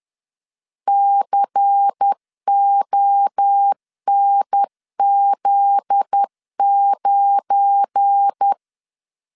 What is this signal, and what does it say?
Morse code, heard as a keyed tone, about 10 wpm, 800 Hz. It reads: CONZ9
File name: LEZIONI / 7 GF9Z / 7 GF9Z 10wpm 2611